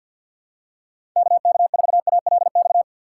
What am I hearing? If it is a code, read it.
Morse code, heard as a keyed tone, about 52 wpm, 700 Hz. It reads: KK4ALX
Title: USA Call Signs - 6 Characters - 6x Character Spacing - 50wpm